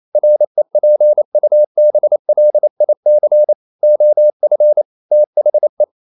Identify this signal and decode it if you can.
Morse code, heard as a keyed tone, about 28 wpm, 600 Hz. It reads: REPUBLIC OF THE
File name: CW / Practice / NationalCapitals28